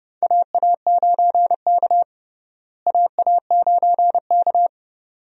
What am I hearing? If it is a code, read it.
Morse code, heard as a keyed tone, about 30 wpm, 700 Hz. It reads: AA9K AA9K
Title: USA Call Signs - 4 Characters - Repeated 2x - 30wpm